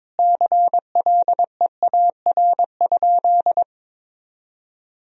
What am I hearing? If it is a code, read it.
Morse code, heard as a keyed tone, about 22 wpm, 700 Hz. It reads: CLEAR?
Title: Sentences from Top 1000 Words - Review Part 4 - 22wpm